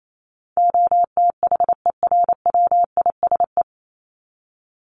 Morse code, heard as a keyed tone, about 28 wpm, 700 Hz. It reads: OTHERWISE
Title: Top 1001-1250 Words 28wpm